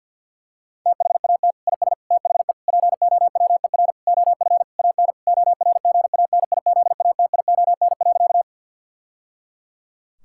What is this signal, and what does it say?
Morse code, heard as a keyed tone, about 50 wpm, 700 Hz. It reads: THAT IS THE POWER OF AN ORGANIZATION.
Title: Sentences from Top 1000 Words - Review Part 5 - 50wpm